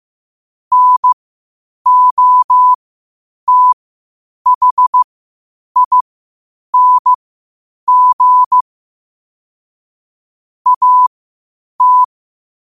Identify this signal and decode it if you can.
Morse code, heard as a keyed tone, about 15 wpm, spaced as for 8 wpm, 1000 Hz. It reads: NOTHING AT
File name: qotdfiles / 8wpm